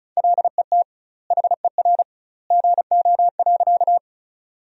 Morse code, heard as a keyed tone, about 34 wpm, 700 Hz. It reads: LET HER GO.
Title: 2x Word-Spacing - Sentences from Top 300 Words - 35wpm